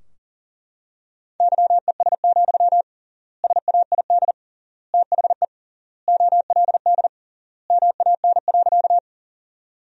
Morse code, heard as a keyed tone, about 37 wpm, 700 Hz. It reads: YES, SAID THE OLD MAN.
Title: 3x Word-Spacing - Sentences from Top 500 Words - 40wpm